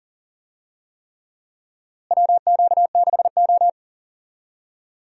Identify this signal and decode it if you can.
Morse code, heard as a keyed tone, about 39 wpm, 700 Hz. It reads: WQ6O
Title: USA Call Signs - 4 Characters - Repeated 3x - 40wpm